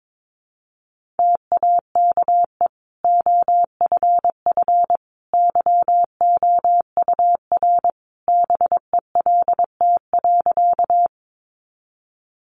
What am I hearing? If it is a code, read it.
Morse code, heard as a keyed tone, about 22 wpm, 700 Hz. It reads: TAKE OFF YOUR BELT.
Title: Sentences from Top 1400 Words 22wpm